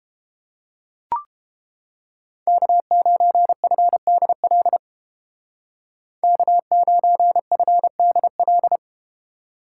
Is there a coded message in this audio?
Yes: Morse code from 2.5 to 8.8 s, heard as a keyed tone, about 31 wpm, 700 Hz. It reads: K9FDL K9FDL